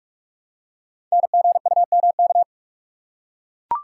Morse code, heard as a keyed tone, about 44 wpm, 690 Hz. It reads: NGUMK
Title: Code Groups - 5 Letters - 45wpm